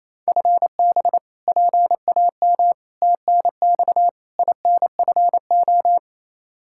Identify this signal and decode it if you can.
Morse code, heard as a keyed tone, about 28 wpm, 700 Hz. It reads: FB PAM TNX INFO
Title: K1USN SST QSO Practice 28wpm